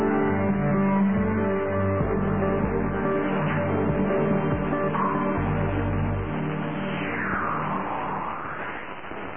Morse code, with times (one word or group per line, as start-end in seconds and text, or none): none